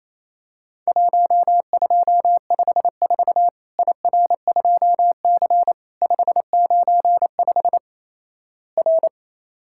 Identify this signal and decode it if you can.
Morse code, heard as a keyed tone, about 27 wpm, 700 Hz. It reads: 1254 IR2C 595 R
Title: Worked All Europe DX Contest - QTC Exchange - 28wpm